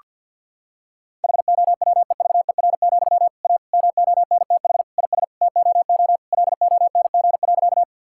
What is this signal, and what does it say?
Morse code, heard as a keyed tone, about 50 wpm, 700 Hz. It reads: HOWEVER, A MONTH IS TOO LONG.
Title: Sentences from Top 1000 Words - Review Part 5 - 50wpm